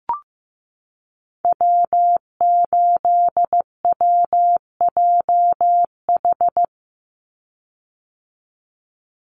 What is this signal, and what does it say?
Morse code, heard as a keyed tone, about 15 wpm, 700 Hz. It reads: W8WJH
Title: USA Call Signs 15wpm